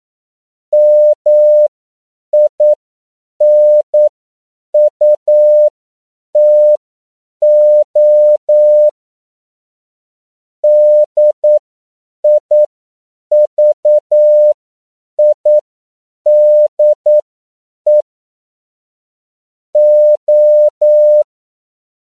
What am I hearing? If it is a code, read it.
Morse code, heard as a keyed tone, about 9 wpm, spaced as for 7 wpm, 600 Hz. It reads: MINUTO DIVIDE O
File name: CW / 10PPM